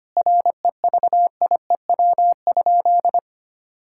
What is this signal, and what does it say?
Morse code, heard as a keyed tone, about 25 wpm, 700 Hz. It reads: REVIEW?
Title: Sentences from Top 1000 Words - Review Part 6 - 25wpm